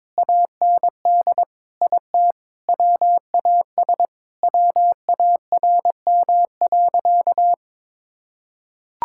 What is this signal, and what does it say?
Morse code, heard as a keyed tone, about 23 wpm, 700 Hz. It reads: AND IT WAS WARM.
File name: practice / 22wpm / Sentences with 2-Word Phrases - 22wpm